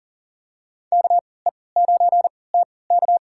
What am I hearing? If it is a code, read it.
Morse code, heard as a keyed tone, about 37 wpm, spaced as for 22 wpm, 700 Hz. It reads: KE9TK
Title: USA Call Signs - 5 Characters - 3x Character Spacing - 40wpm